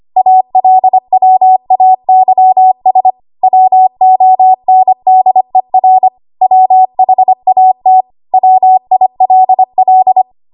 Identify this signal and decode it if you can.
Morse code, heard as a keyed tone, about 25 wpm, 750 Hz. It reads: ALWAYS WONDER WHAT WILL
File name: Archive / 25 WPM / 25WPM